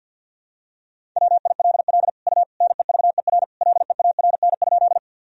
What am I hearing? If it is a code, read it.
Morse code, heard as a keyed tone, about 52 wpm, 700 Hz. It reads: WILL U NEVER LEARN?
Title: Sentences from Top 300 Words and Abbreviations 50wpm